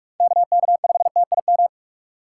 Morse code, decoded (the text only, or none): KK5TIM